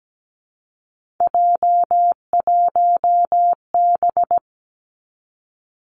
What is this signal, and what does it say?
Morse code, heard as a keyed tone, about 17 wpm, 700 Hz. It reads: J1B
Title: Three Challenging Letter-Number Combinations 17wpm